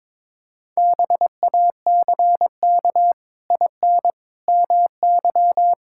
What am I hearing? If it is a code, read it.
Morse code, heard as a keyed tone, about 22 wpm, 700 Hz. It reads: BACK IN MY
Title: Pangram Challenge v2 - 22wpm